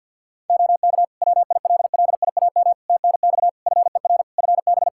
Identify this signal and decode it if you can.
Morse code, heard as a keyed tone, about 49 wpm, 700 Hz. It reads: OK WILLIAM TNX FER FB
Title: Practice QSO - Exchange 4 - Ending - 50wpm